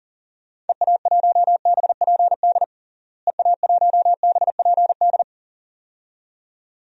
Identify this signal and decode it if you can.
Morse code, heard as a keyed tone, about 40 wpm, 700 Hz. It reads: EA1BPD EA1BPD
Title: DXCC-100 Call Signs - 6 Characters - Repeated 2x - 40wpm